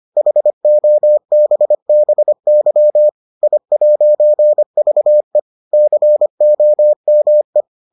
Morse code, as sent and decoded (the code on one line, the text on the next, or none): .... --- -... -... -.-- / .. .----. ...- . / -.-. --- -- .
HOBBY I'VE COME